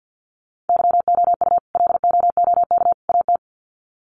Morse code, heard as a keyed tone, about 50 wpm, 700 Hz. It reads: YOU LOOK AT
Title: Sets of 3 Words 50wpm